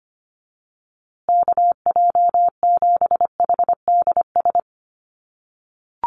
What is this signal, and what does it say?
Morse code, heard as a keyed tone, about 25 wpm, 700 Hz. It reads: KJ7HDS
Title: USA Call Signs Repeated 3x 25wpm